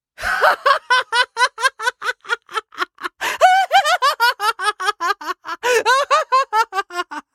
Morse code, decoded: none